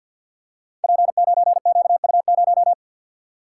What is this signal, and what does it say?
Morse code, heard as a keyed tone, about 53 wpm, 700 Hz. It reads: P9QU0
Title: Code Groups - 5 Letter:Numbers - 50wpm